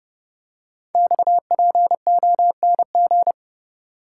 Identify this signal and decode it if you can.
Morse code, heard as a keyed tone, about 30 wpm, 700 Hz. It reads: XPONG
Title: Code Groups - 5 Letters - 30wpm